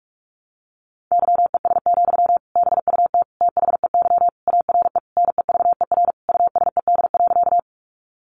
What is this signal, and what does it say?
Morse code, heard as a keyed tone, about 44 wpm, 700 Hz. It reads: YES, BUT THEY ARE NEVER USED.
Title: Sentences from Top 600 Words 45wpm